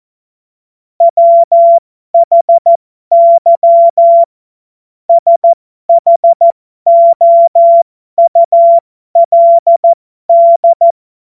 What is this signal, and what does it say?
Morse code, heard as a keyed tone, about 14 wpm, spaced as for 12 wpm, 680 Hz. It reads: WHY SHOULD
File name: Morsekurs / Witze